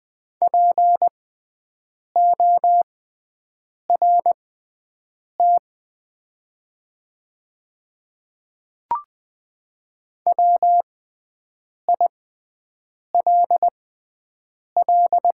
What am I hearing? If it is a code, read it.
Morse code, heard as a keyed tone, about 20 wpm, spaced as for 7 wpm, 700 Hz. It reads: PORT WILL
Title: 6x Character Spacing - Top 300 Words - 20wpm